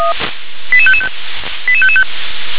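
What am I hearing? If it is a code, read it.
Five-tone selective call (ZVEI), heard as quick runs of five tones, tones of about 70 ms. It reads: 88555 88555
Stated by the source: ZVEI five-tone selective call received on an SDR.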